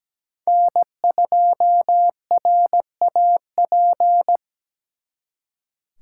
Morse code, heard as a keyed tone, about 20 wpm, 700 Hz. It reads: N2RAP
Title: Speed Racing - USA Call Signs 17wpm